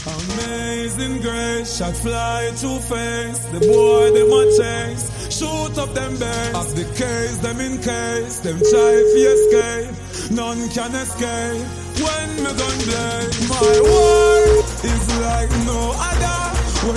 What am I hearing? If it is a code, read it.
Ringback tone (Europe, 425 Hz), with noise.